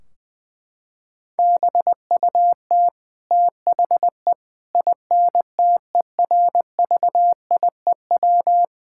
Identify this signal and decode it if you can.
Morse code, heard as a keyed tone, about 20 wpm, 700 Hz. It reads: BUT THE INTERVIEW
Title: Sentences from Top 1600 Words 20wpm